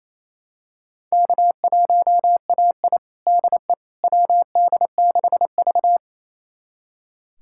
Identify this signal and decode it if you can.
Morse code, heard as a keyed tone, about 28 wpm, 700 Hz. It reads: K1AI DE WD6V
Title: USA Callsign DE Callsign - 4 Characters - 28wpm